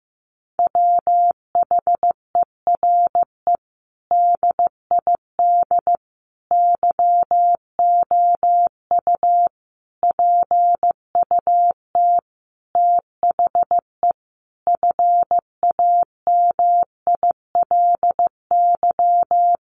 Morse code, heard as a keyed tone, about 15 wpm, 700 Hz. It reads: WHERE DID YOU PUT THE FAMILY
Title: Sentences from Top 1200 Words 15wpm